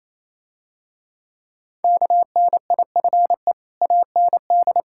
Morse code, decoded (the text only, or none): KNIFE AND